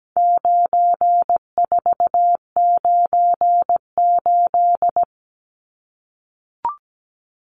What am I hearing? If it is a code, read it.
Morse code, heard as a keyed tone, about 17 wpm, 700 Hz. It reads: 9498
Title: Numbers - 4-Digit 17wpm